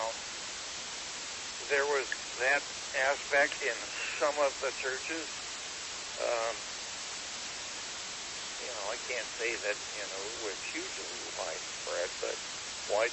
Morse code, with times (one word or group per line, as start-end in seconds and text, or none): none